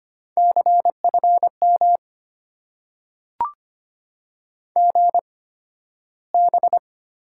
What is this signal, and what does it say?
Morse code, heard as a keyed tone, about 25 wpm, spaced as for 13 wpm, 700 Hz. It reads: CFM GB
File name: Top 90 QSO Elements - 8x Character Spacing - 25wpm